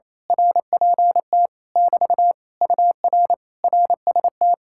Morse code, heard as a keyed tone, about 26 wpm, 700 Hz. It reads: RPT <BT> UR RST
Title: Speed-Racing Common QSO Phrases Encore - 28wpm